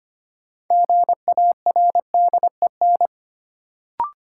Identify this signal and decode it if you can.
Morse code, heard as a keyed tone, about 25 wpm, 700 Hz. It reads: GARDEN
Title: Sentences from Top 700 Words - New Word Drill - 25wpm